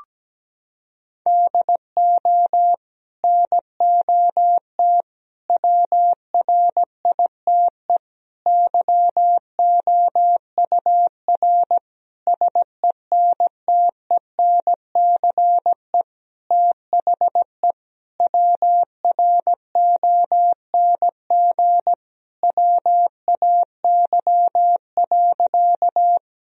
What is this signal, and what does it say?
Morse code, heard as a keyed tone, about 17 wpm, 700 Hz. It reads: DO NOT WRITE YOUR SENTENCE THE WRONG WAY.
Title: Sentences from Top 1000 Words - Review Part 1 - 17wpm